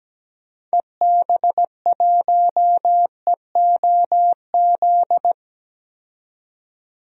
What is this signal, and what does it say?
Morse code, heard as a keyed tone, about 17 wpm, 700 Hz. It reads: EB1EOZ
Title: DXCC-100 Call Signs - 6 Characters - 17wpm